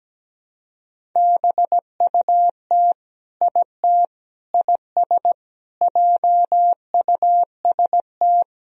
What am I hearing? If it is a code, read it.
Morse code, heard as a keyed tone, about 17 wpm, 700 Hz. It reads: BUT IT IS JUST
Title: Sentences from Top 200 Words and Abbreviations 17wpm